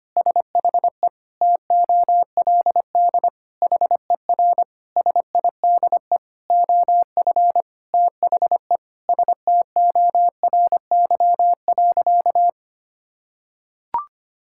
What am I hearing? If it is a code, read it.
Morse code, heard as a keyed tone, about 25 wpm, 700 Hz. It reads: SHE TOLD HER SIDE OF THE STORY.
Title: Sentences from Top 1000 Words - Review Part 6 - 25wpm